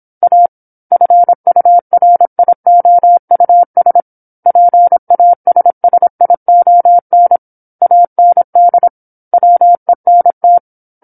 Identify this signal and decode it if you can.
Morse code, heard as a keyed tone, about 27 wpm, 700 Hz. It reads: A FURIOUS PASSION AND WENT
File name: 08-Chapter-08 26 WPM